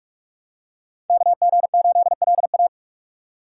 Morse code, heard as a keyed tone, about 40 wpm, 700 Hz. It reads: KG8LA